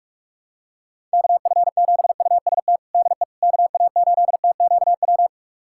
Morse code, heard as a keyed tone, about 46 wpm, 700 Hz. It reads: KF7UST DE KA8TQW
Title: USA Callsign DE Callsign 45wpm